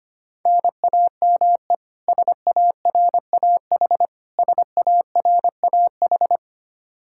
Morse code, heard as a keyed tone, about 25 wpm, 700 Hz. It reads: NAME SARAH SARAH
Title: Practice QSO - Exchange 1 - Beginning - 25wpm